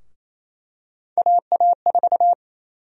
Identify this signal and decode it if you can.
Morse code, heard as a keyed tone, about 27 wpm, 700 Hz. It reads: AA4
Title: USA Call Sign Prefixes - 28wpm